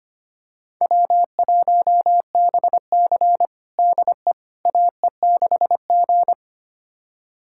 Morse code, heard as a keyed tone, about 25 wpm, 700 Hz. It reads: W1BC DE AE6G
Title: USA Callsign DE Callsign - 4 Characters - 25wpm